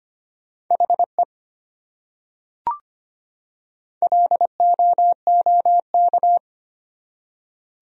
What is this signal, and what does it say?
Morse code, heard as a keyed tone, about 25 wpm, 700 Hz. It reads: HE LOOK
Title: Top 100 Words 25wpm